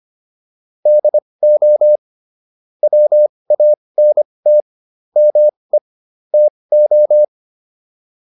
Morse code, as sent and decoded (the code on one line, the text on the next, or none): -.. --- / .-- .- -. - / -- . / - ---
DO WANT ME TO